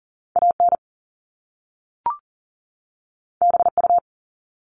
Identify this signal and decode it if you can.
Morse code, heard as a keyed tone, about 40 wpm, 700 Hz. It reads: AN BU
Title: Top Two Letter Combinations in Words 40wpm